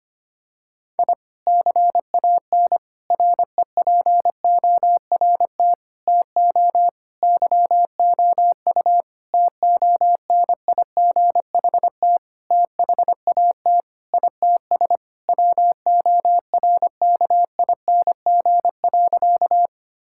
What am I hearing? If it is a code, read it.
Morse code, heard as a keyed tone, about 25 wpm, 700 Hz. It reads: I CAN REPORT TO YOU TONIGHT THAT ITS WORKING.
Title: Sentences from Top 1000 Words - Review Part 6 - 25wpm